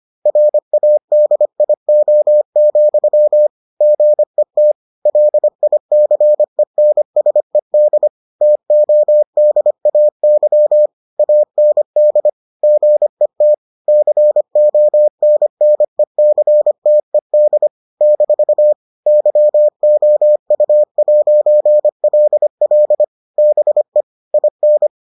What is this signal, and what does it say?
Morse code, heard as a keyed tone, about 25 wpm, 600 Hz. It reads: RADIO, GET LICENSED TODAY AND GET CONNECTED - YOU'LL BE IN